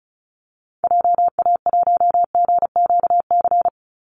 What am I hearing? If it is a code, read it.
Morse code, heard as a keyed tone, about 34 wpm, 700 Hz. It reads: JA1GQC
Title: Call Signs - 35wpm